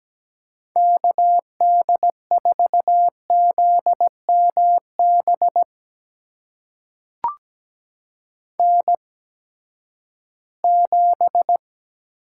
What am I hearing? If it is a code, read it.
Morse code, heard as a keyed tone, about 17 wpm, spaced as for 12 wpm, 700 Hz. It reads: KD4ZMB N7
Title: USA Call Signs - 8x Character Spacing - 17wpm